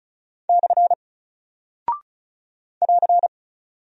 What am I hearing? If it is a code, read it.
Morse code, heard as a keyed tone, about 36 wpm, 700 Hz. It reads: / <AR>
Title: Single Character (Letter:Number:Punctuation:Prosign) Alt 35wpm